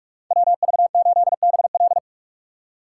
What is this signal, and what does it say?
Morse code, heard as a keyed tone, about 44 wpm, 700 Hz. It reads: WV8BL
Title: USA Call Signs - 4x Character Spacing - 45wpm